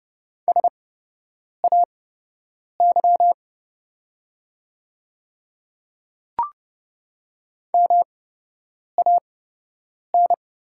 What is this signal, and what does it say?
Morse code, heard as a keyed tone, about 29 wpm, spaced as for 8 wpm, 700 Hz. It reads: SAY MAN
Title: 8x Character Spacing - Top 300 Words - 30wpm